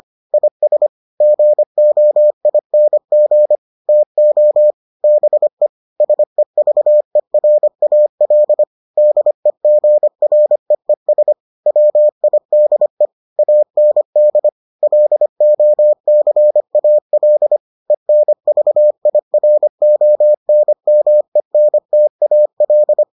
Morse code, heard as a keyed tone, about 25 wpm, 600 Hz. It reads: IS GOING TO BE SEVERAL DEGREES WIDE AND LOCAL ENVIRONMENTAL